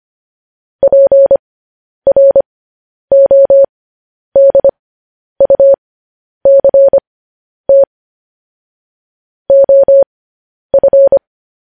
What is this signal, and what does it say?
Morse code, heard as a keyed tone, about 25 wpm, spaced as for 10 wpm, 550 Hz. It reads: PRODUCT OF
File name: podcast / News headlines-1x-25x10